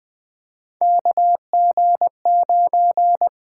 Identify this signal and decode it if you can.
Morse code, heard as a keyed tone, about 20 wpm, 700 Hz. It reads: KG9
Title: USA Call Signs - 5 Characters - 8x Character Spacing - 20wpm